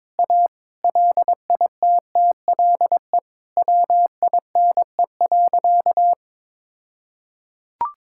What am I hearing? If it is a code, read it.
Morse code, heard as a keyed tone, about 22 wpm, 700 Hz. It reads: A LITTLE WINE.
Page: morse-code-ninja-practice-22wpm